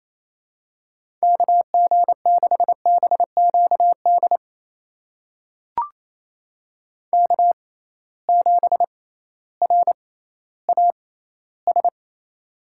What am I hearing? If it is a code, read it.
Morse code, heard as a keyed tone, about 28 wpm, spaced as for 15 wpm, 700 Hz. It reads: KG6BQD K7RAS
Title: USA Call Signs - 6x Character Spacing - 28wpm